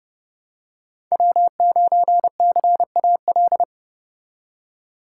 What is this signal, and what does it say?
Morse code, heard as a keyed tone, about 29 wpm, 700 Hz. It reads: W9CAL